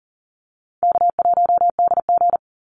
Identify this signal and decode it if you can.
Morse code, heard as a keyed tone, about 40 wpm, 700 Hz. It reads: K1DG
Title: Easy Call Signs Repeated 3x - 40wpm